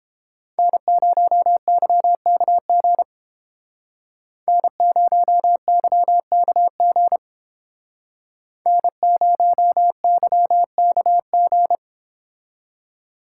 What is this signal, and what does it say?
Morse code, heard as a keyed tone, about 30 wpm, 700 Hz. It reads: N0YKG N0YKG N0YKG